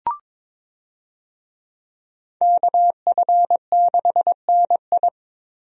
Morse code, heard as a keyed tone, about 21 wpm, 700 Hz. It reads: KF6NI
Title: USA Call Signs - 5 Characters - Repeated 3x - 22wpm